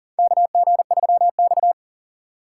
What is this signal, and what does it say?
Morse code, heard as a keyed tone, about 40 wpm, 700 Hz. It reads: KG3X
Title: USA Call Signs - 4 Characters - 4x Character Spacing - 40wpm